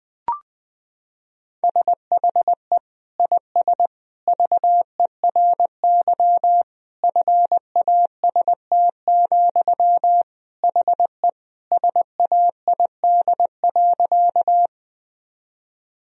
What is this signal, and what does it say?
Morse code, heard as a keyed tone, about 20 wpm, 700 Hz. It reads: SHE IS VERY FAST, HE SAID.
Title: Sentences from Top 1000 Words - Review Part 2 - 20wpm